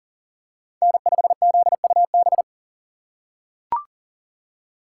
Morse code, heard as a keyed tone, about 40 wpm, 700 Hz. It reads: N5ZUB
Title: USA Call Signs - 5 Characters - 3x Character Spacing - 40wpm